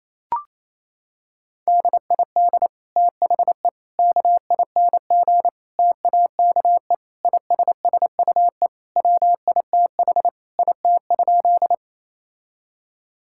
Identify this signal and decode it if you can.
Morse code, heard as a keyed tone, about 28 wpm, 700 Hz. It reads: DID THE KING TAKE ISSUE WITH IT?
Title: Sentences from Top 1000 Words - Review Part 6 - 28wpm